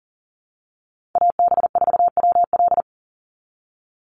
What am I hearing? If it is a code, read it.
Morse code, heard as a keyed tone, about 40 wpm, 700 Hz. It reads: AB4WL
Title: NA SOTA Chaser Callsigns 40wpm